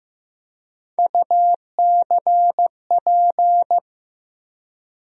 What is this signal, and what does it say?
Morse code, heard as a keyed tone, about 15 wpm, 700 Hz. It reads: UCP
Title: USA Call Sign Postfixes - 3 Letters 15wpm